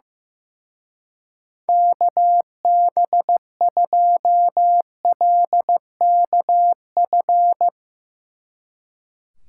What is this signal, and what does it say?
Morse code, heard as a keyed tone, about 15 wpm, 700 Hz. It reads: KB2LKF
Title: Speed Racing - USA 2x3 Call Signs 15wpm